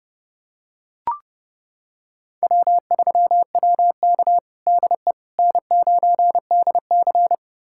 Morse code, heard as a keyed tone, about 32 wpm, 700 Hz. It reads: W3WK DE N9DC
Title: USA Callsign DE Callsign - 4 Characters - 30wpm